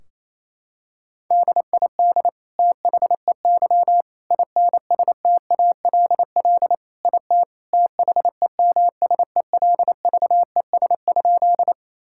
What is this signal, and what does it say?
Morse code, heard as a keyed tone, about 27 wpm, 700 Hz. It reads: DID THEY INSTALL IT THEMSELVES?
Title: Sentences from Top 1600 Words 28wpm